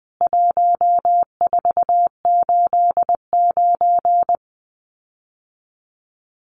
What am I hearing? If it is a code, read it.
Morse code, heard as a keyed tone, about 20 wpm, 700 Hz. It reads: 1489
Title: CWT Exchange 20wpm